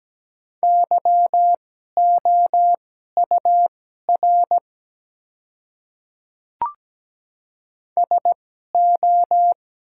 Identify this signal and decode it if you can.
Morse code, heard as a keyed tone, about 17 wpm, spaced as for 12 wpm, 700 Hz. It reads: YOUR SO
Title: 2x Character Spacing - Top 100 Words - 17wpm